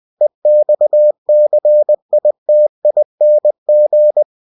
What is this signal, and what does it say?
Morse code, heard as a keyed tone, about 20 wpm, 600 Hz. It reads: EXCITING
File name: cw-books / winnie-the-pooh / winnie-20wpm-page-0110